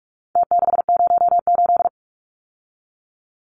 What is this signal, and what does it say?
Morse code, heard as a keyed tone, about 44 wpm, 700 Hz. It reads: T608
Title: Numbers - Contest Style - 4-Digit 45wpm